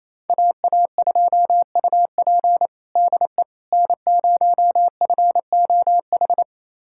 Morse code, heard as a keyed tone, about 28 wpm, 700 Hz. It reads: AA2UP DE N0FOH
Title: USA Callsign DE Callsign - 5 Characters - 28wpm